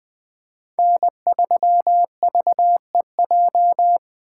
Morse code, heard as a keyed tone, about 20 wpm, 700 Hz. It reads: N3VEJ
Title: 2023 POTA Activator Call Signs - 20wpm